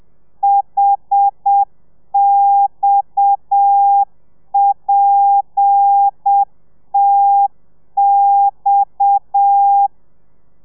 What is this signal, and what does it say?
Morse code, heard as a keyed tone, about 7 wpm, 800 Hz. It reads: HXPTX
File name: telegrafia / i2tfj / Lez7 30